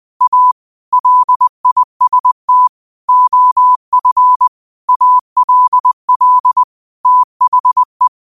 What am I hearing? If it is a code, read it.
Morse code, heard as a keyed tone, about 20 wpm, 1000 Hz. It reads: A LIST OF ALL THE